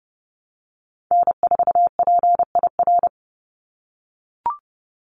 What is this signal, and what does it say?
Morse code, heard as a keyed tone, about 30 wpm, 700 Hz. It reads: N4PIR
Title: Call Signs - 30wpm